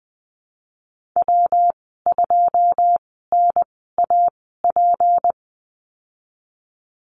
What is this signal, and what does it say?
Morse code, heard as a keyed tone, about 20 wpm, spaced as for 15 wpm, 700 Hz. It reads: W2NAP
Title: 161 Lesson 26g - Letter g - Call Signs 20wpm